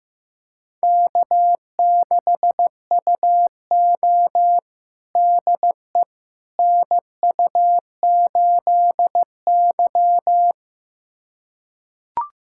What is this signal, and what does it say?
Morse code, heard as a keyed tone, about 15 wpm, 700 Hz. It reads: K6UO DE NU8Y
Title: USA Callsign DE Callsign - 4 Characters - 15wpm